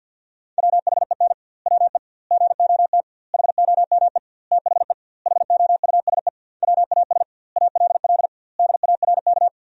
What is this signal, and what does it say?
Morse code, heard as a keyed tone, about 47 wpm, 700 Hz. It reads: WHEN WE GOT HOME THE HOUSE WAS ALL DARK
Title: 2x Word-Spacing - Sentences from Top 500 Words - 50wpm